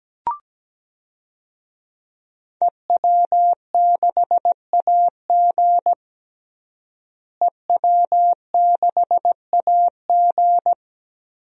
Morse code, heard as a keyed tone, about 17 wpm, 700 Hz. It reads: EW6AG EW6AG
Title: DXCC-100 Call Signs - 5 Characters - Repeated 3x - 17wpm